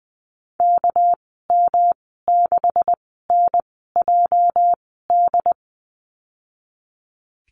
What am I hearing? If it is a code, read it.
Morse code, heard as a keyed tone, about 20 wpm, spaced as for 14 wpm, 700 Hz. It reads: KM6NJD
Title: 219 Lesson 35g - Number 0 - Call Signs 20wpm